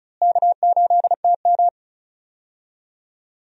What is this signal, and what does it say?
Morse code, heard as a keyed tone, about 36 wpm, 700 Hz. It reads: K8TM
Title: USA Call Signs - 4 Characters - Repeated 2x - 35wpm